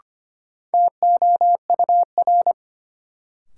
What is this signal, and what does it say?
Morse code, heard as a keyed tone, about 25 wpm, 700 Hz. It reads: TOUR
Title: Sentences from Top 1400 Words - New Word Drill Alt - 25wpm